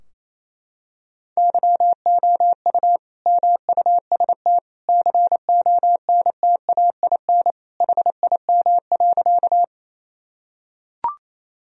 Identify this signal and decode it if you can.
Morse code, heard as a keyed tone, about 28 wpm, 700 Hz. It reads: YOU MUST CONTAIN HIM.
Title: Sentences from Top 1000 Words - Review Part 2 - 28wpm